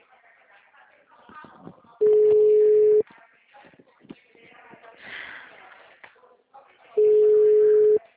Ringback tone (Europe, 425 Hz).